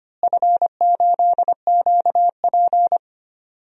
Morse code, heard as a keyed tone, about 25 wpm, 700 Hz. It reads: F8QP
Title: DXCC-100 Call Signs - 4 Characters - Repeated 3x - 25wpm